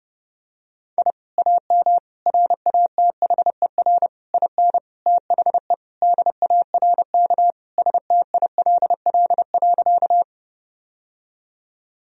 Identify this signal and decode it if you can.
Morse code, heard as a keyed tone, about 30 wpm, 700 Hz. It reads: I AM RATHER IN THE DARK STILL.
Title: Sentences from Top 1000 Words - Review Part 4 - 30wpm